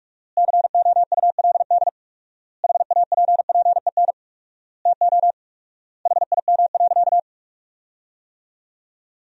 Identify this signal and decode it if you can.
Morse code, heard as a keyed tone, about 46 wpm, 700 Hz. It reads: COULD HAPPEN TO HIM.
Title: 4x Word-Spacing - Sentences from Top 400 Words - 45wpm